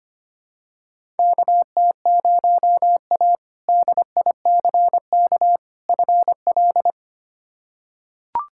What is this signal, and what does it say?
Morse code, heard as a keyed tone, about 25 wpm, 700 Hz. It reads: KT0A DICK FL
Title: K1USN SST QSO Practice - 25wpm